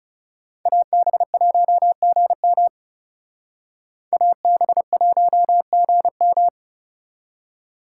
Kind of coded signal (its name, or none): Morse code